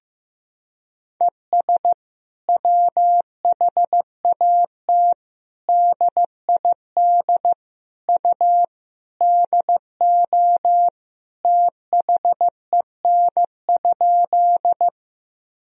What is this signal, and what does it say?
Morse code, heard as a keyed tone, about 16 wpm, 700 Hz. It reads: ES WHAT DID U DO THEN?
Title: Sentences from Top 100 Words and Abbreviations 15wpm